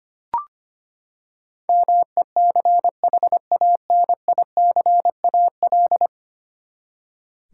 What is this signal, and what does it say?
Morse code, heard as a keyed tone, about 25 wpm, 700 Hz. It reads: MECHANICAL
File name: Ten Letter Words - 25wpm